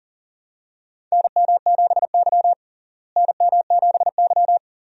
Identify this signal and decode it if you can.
Morse code, heard as a keyed tone, about 40 wpm, 700 Hz. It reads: NM7Y NM7Y
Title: USA Call Signs - 4 Characters - Repeated 2x - 40wpm